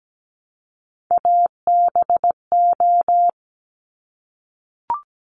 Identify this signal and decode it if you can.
Morse code, heard as a keyed tone, about 17 wpm, 700 Hz. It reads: ABO
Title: Top Three Letter Combinations in Words 17wpm